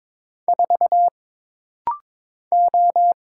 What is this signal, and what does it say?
Morse code, heard as a keyed tone, about 22 wpm, 700 Hz. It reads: 4 O
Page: This rapid-fire timing is geared to push your ICR (Instant Character Recognition) skills.